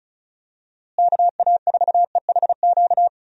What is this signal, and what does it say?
Morse code, heard as a keyed tone, about 34 wpm, 700 Hz. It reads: KA4EHQ
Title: USA Call Signs - 6 Characters - Repeated 3x - 35wpm